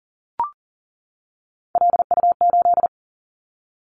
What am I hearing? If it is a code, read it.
Morse code, heard as a keyed tone, about 38 wpm, 700 Hz. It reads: LU8
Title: Callsign Prefixes for Top 50 DXCC - 40wpm